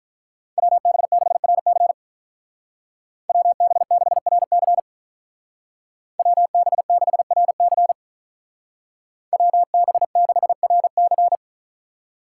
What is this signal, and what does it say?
Morse code, heard as a keyed tone, about 47 wpm, 700 Hz. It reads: WB6RC WB6RC WB6RC WB6RC